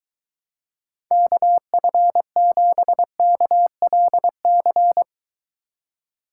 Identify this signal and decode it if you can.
Morse code, heard as a keyed tone, about 23 wpm, 700 Hz. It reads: KF7KLC